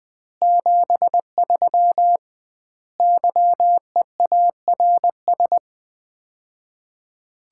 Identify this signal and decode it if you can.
Morse code, heard as a keyed tone, about 21 wpm, 700 Hz. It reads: 73 YEARS
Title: Common QSO Phrases - 2x Word-Spacing - 20wpm